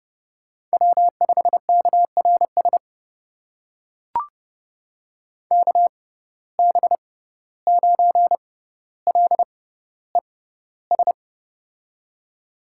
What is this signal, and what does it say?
Morse code, heard as a keyed tone, about 30 wpm, spaced as for 15 wpm, 700 Hz. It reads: W5KRS KB9LES
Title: USA Call Signs - 6x Character Spacing - 30wpm